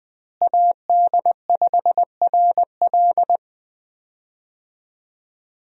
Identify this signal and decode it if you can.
Morse code, heard as a keyed tone, about 20 wpm, 700 Hz. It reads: AD5RL
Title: USA Call Signs - 5 Characters - Repeated 3x - 20wpm